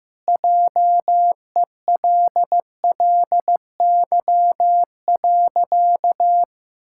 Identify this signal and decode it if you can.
Morse code, heard as a keyed tone, about 15 wpm, 700 Hz. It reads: JELLY.
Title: Pangram Challenge v2 - 15wpm